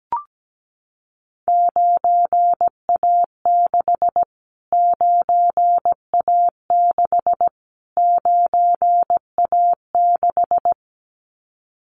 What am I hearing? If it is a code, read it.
Morse code, heard as a keyed tone, about 17 wpm, 700 Hz. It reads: 9A6 9A6 9A6
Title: Callsign Prefixes for Top 25 DXCC - Repeated 3x - 17wpm